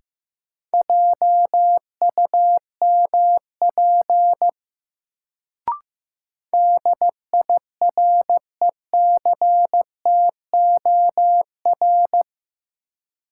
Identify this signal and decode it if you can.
Morse code, heard as a keyed tone, about 16 wpm, 700 Hz. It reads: JUMP DIRECTOR
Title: Sentences from Top 1300 Words - New Word Drill - 15wpm